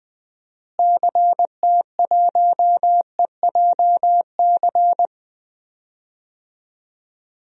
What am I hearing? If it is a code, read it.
Morse code, heard as a keyed tone, about 20 wpm, 700 Hz. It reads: CT1EJC
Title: DXCC-100 Call Signs - 6 Characters - Repeated 2x - 20wpm